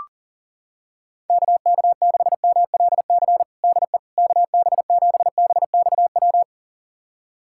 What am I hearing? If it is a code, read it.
Morse code, heard as a keyed tone, about 37 wpm, 700 Hz. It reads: KK6MLC DE KB7BXW
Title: USA Callsign DE Callsign 40wpm